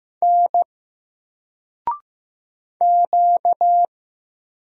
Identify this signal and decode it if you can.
Morse code, heard as a keyed tone, about 15 wpm, 700 Hz. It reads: N Q
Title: Single Character (Letter:Number:Punctuation:Prosign) 15wpm